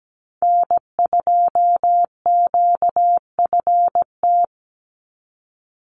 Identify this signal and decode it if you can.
Morse code, heard as a keyed tone, about 17 wpm, 700 Hz. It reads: N2QFT
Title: USA Call Signs 17wpm